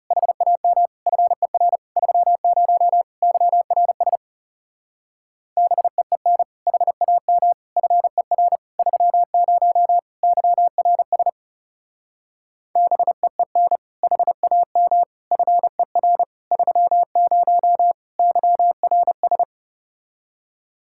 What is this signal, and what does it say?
Morse code, heard as a keyed tone, about 36 wpm, 700 Hz. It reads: HAM FER 30 YRS BEEN HAM FER 30 YRS BEEN HAM FER 30 YRS
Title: Speed-Racing Common QSO Phrases Encore - 30wpm